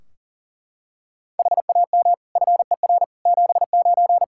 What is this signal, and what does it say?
Morse code, heard as a keyed tone, about 40 wpm, 700 Hz. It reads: HAM FER 79
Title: Practice QSO - Exchange 3 - More Info - 40wpm